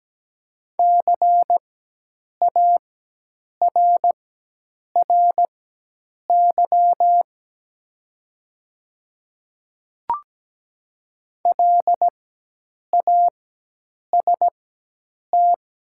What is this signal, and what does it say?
Morse code, heard as a keyed tone, about 17 wpm, spaced as for 8 wpm, 700 Hz. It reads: CARRY LAST
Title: 4x Character Spacing - Top 300 Words - 17wpm